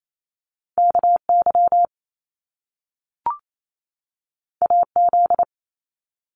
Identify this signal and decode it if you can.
Morse code, heard as a keyed tone, about 28 wpm, 700 Hz. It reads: KY AZ
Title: US States 28wpm